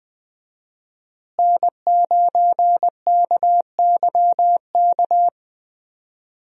Morse code, heard as a keyed tone, about 20 wpm, 700 Hz. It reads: N9KYK
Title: USA Call Signs - 5 Characters - Repeated 2x - 20wpm